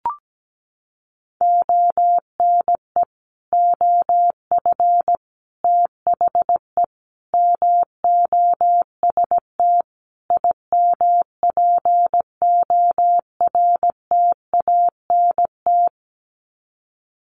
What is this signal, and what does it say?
Morse code, heard as a keyed tone, about 17 wpm, 700 Hz. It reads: ONE OF THE MOST IMPORTANT
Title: Sets of 5 Words 17wpm